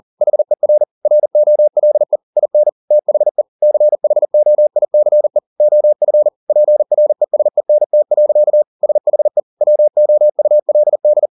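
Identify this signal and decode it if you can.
Morse code, heard as a keyed tone, about 40 wpm, 600 Hz. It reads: HER ROLE IN THE CHOICE OF PRESENT. SHE WOULD
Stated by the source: Morse code at 40 wpm, practice text, CW med Gnister.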